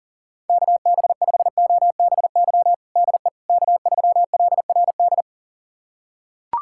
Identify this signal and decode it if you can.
Morse code, heard as a keyed tone, about 40 wpm, 700 Hz. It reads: KB5OBY DE K3LRD
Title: USA Callsign DE Callsign 40wpm